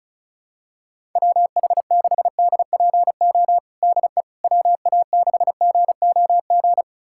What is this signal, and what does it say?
Morse code, heard as a keyed tone, about 34 wpm, 700 Hz. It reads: WH6DPO DE WA6GOG
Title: USA Callsign DE Callsign 35wpm